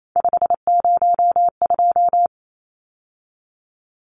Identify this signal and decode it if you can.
Morse code, heard as a keyed tone, about 28 wpm, 700 Hz. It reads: 502
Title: Numbers - 3-Digit 28wpm